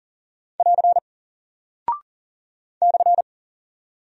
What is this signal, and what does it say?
Morse code, heard as a keyed tone, about 40 wpm, 700 Hz. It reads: <AR> /
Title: Single Character (Letter:Number:Punctuation:Prosign) Alt 40wpm